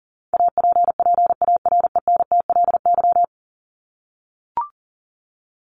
Morse code, heard as a keyed tone, about 40 wpm, 700 Hz. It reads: APPARENTLY
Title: LY Suffix - Top 100 Words with Base Words 40wpm